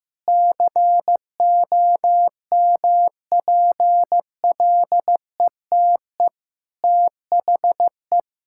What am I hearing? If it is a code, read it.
Morse code, heard as a keyed tone, about 15 wpm, 700 Hz. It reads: COMPLETE THE
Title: Sentences from Top 1000 Words - Review Part 6 - 15wpm